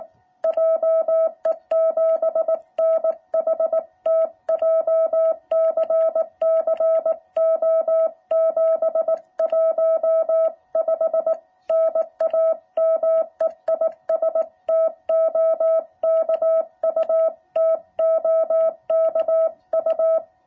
Morse code, heard as a keyed tone, about 18 wpm, 660 Hz. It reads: JE7NHT JCC O715 NAME IS TOKU TOKU